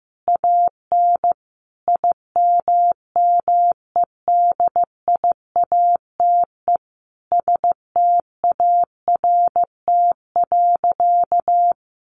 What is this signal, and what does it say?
Morse code, heard as a keyed tone, about 15 wpm, 700 Hz. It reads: AN IMMEDIATE START.
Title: Sentences from Top 1200 Words 15wpm